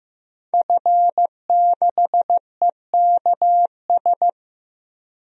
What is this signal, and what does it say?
Morse code, heard as a keyed tone, about 15 wpm, 700 Hz. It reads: F6EKS
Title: DXCC-100 Call Signs - 5 Characters - 15wpm